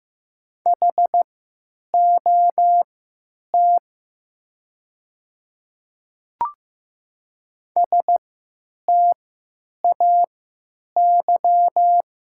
Morse code, heard as a keyed tone, about 15 wpm, spaced as for 9 wpm, 700 Hz. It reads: HOT STAY
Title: 3x Character Spacing - Top 500 Words - 15wpm